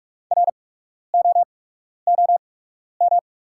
Morse code, heard as a keyed tone, about 44 wpm, spaced as for 12 wpm, 690 Hz. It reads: ROOM
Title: 8x Character Spacing - Top 300 Words - 45wpm